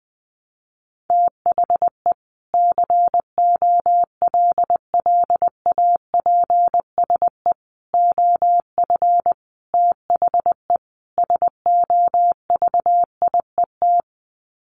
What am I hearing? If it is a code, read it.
Morse code, heard as a keyed tone, about 20 wpm, 700 Hz. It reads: THE COLLAPSE OF THE SOVIET
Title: Sets of 5 Words Encore 20wpm